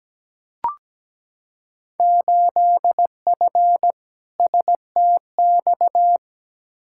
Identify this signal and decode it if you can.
Morse code, heard as a keyed tone, about 17 wpm, 700 Hz. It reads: 8F STX
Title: Field Day Exchanges Encore - 17wpm